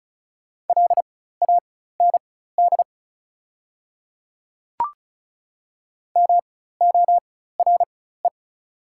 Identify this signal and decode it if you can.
Morse code, heard as a keyed tone, about 36 wpm, spaced as for 16 wpm, 690 Hz. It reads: LAND MORE
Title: 4x Character Spacing - Top 200 Words - 35wpm